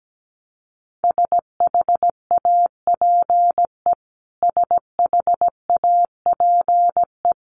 Morse code, heard as a keyed tone, about 17 wpm, 700 Hz. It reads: SHAPE SHAPE
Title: Top 500 Words Repeated 17wpm